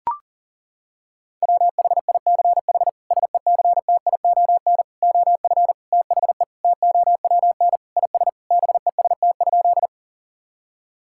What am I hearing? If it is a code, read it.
Morse code, heard as a keyed tone, about 40 wpm, 700 Hz. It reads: WHICH SECTION OF THE TOWN IS BEST?
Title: Sentences from Top 1000 Words - Review Part 5 - 40wpm